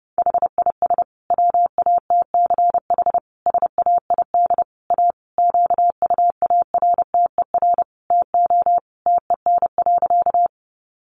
Morse code, heard as a keyed tone, about 30 wpm, 700 Hz. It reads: HIS WATCH SAID A QUARTER TO TEN.
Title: Sentences from Top 1300 Words 30wpm